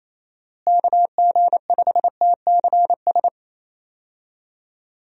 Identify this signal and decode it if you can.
Morse code, heard as a keyed tone, about 28 wpm, 700 Hz. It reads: KG5TCS
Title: USA Call Signs - 6 Characters - Repeated 2x - 28wpm